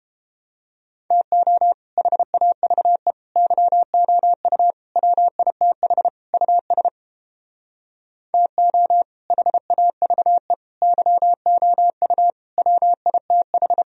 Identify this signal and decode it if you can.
Morse code, heard as a keyed tone, about 31 wpm, 700 Hz. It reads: TO HAVE YOU WITH US TO HAVE YOU WITH